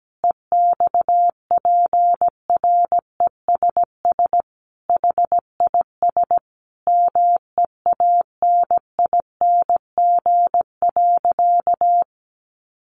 Morse code, heard as a keyed tone, about 17 wpm, 700 Hz. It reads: EXPRESS HIS MEANING.
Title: Sentences from Top 1100 Words 17wpm